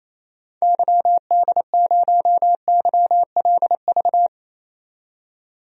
Morse code, heard as a keyed tone, about 28 wpm, 700 Hz. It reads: YD0YLV
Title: DXCC-100 Call Signs - 6 Characters - 28wpm